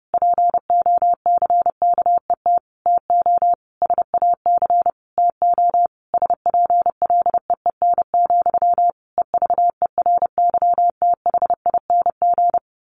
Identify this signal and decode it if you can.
Morse code, heard as a keyed tone, about 30 wpm, 700 Hz. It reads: POCKET TO SAC TO SPLEEN, EVERYTHING
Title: A Sound Of Thunder 30wpm